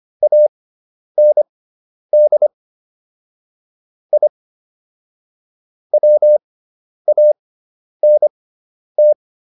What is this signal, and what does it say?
Morse code, heard as a keyed tone, about 25 wpm, spaced as for 10 wpm, 600 Hz. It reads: AND I WANT